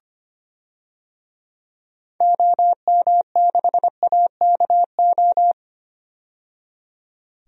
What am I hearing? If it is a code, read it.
Morse code, heard as a keyed tone, about 25 wpm, 700 Hz. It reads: OM6AKO
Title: DXCC-100 Call Signs - 6 Characters - 25wpm